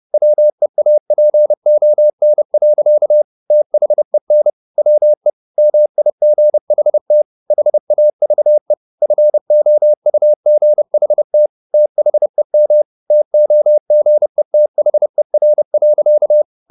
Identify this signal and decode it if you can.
Morse code, heard as a keyed tone, about 30 wpm, 600 Hz. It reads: WEAPON. THEN WE MIGHT HAVE FOUGHT THEM TOGETHER.